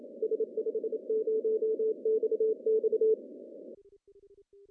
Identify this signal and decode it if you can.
Morse code, heard as a keyed tone, about 28 wpm, 440 Hz. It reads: S50XX
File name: CW / cqww-cw-k5tr